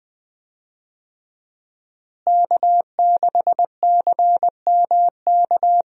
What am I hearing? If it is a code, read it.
Morse code, heard as a keyed tone, about 20 wpm, 700 Hz. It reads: K6CMK
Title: USA Call Signs - 5 Characters - Repeated 3x - 20wpm